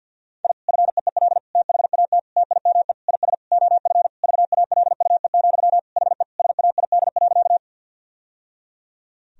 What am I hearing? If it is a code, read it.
Morse code, heard as a keyed tone, about 49 wpm, 700 Hz. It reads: I FEEL THAT TIME IS OF VALUE, HE SAID.
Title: Sentences from Top 1000 Words - Review Part 3 - 50wpm